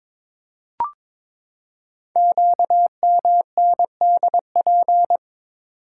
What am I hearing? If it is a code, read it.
Morse code, heard as a keyed tone, about 22 wpm, 700 Hz. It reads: QMNDP
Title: Code Groups - 5 Letters - 22wpm